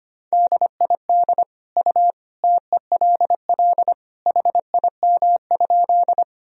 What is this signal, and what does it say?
Morse code, heard as a keyed tone, about 25 wpm, 700 Hz. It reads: DID U TELL HIM?
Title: Sentences from Top 200 Words and Abbreviations 25wpm